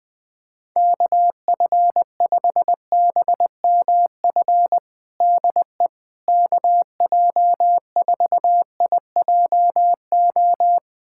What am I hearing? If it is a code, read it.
Morse code, heard as a keyed tone, about 21 wpm, 700 Hz. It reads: KF5BMF DE KJ4IJO
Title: USA Callsign DE Callsign - 6 Characters - 20wpm